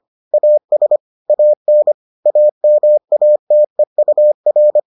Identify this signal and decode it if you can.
Morse code, heard as a keyed tone, about 25 wpm, 600 Hz. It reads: AS AN AMATEUR